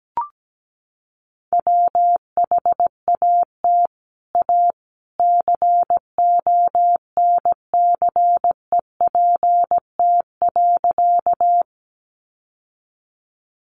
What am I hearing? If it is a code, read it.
Morse code, heard as a keyed tone, about 17 wpm, 700 Hz. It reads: WHAT A CONCEPT.
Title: Sentences from Top 1100 Words 17wpm